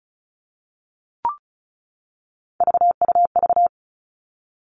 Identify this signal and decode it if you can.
Morse code, heard as a keyed tone, about 36 wpm, 700 Hz. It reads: VUV
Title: Three Challenging Letter-Number Combinations 35wpm